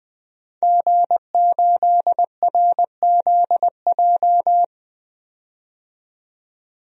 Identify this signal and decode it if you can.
Morse code, heard as a keyed tone, about 20 wpm, 700 Hz. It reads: G8RZJ
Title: DXCC-100 Call Signs - 5 Characters - 20wpm